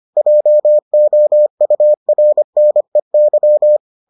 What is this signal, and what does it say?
Morse code, heard as a keyed tone, about 25 wpm, 600 Hz. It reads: JOURNEY